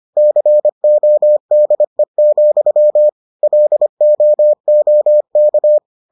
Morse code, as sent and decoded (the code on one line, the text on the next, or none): -.-. --- -.. --..-- / .-.. --- --- -.-
CODE, LOOK